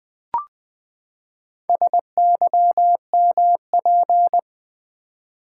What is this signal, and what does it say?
Morse code, heard as a keyed tone, about 20 wpm, 700 Hz. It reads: SYMP
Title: Build A Word - 4 Letters At A Time - 11-Letter Words - 20wpm